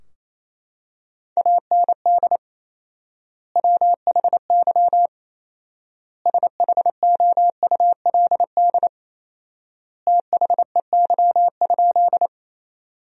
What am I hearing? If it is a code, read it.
Morse code, heard as a keyed tone, about 28 wpm, 700 Hz. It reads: AND WHY SHOULD THEY?
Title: 4x Word-Spacing - Sentences from Top 300 Words - 28wpm